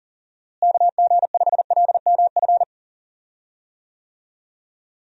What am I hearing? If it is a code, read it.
Morse code, heard as a keyed tone, about 37 wpm, 700 Hz. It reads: KG5LMF
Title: USA Call Signs - 6 Characters - Repeated 2x - 40wpm